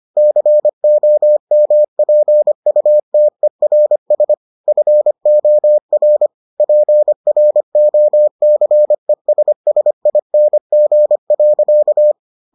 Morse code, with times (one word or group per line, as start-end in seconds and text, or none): 0.2-4.3 COMPUTERS
4.7-6.3 FOR
6.6-12.1 PROCESSING.